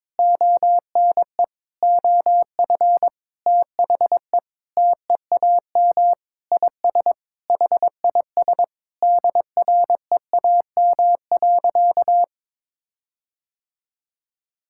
Morse code, heard as a keyed tone, about 22 wpm, 700 Hz. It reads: ONE OF THE TEAM IS HIS DREAM.
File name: practice / 22wpm / Sentences with 4-Word Phrases - 22wpm